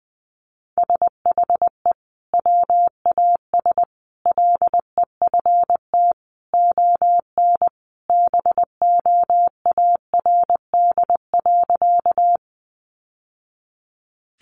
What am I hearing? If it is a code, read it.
Morse code, heard as a keyed tone, about 20 wpm, 700 Hz. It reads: SHE WAS LEFT ON BOARD.
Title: Sentences from Top 700 Words 20wpm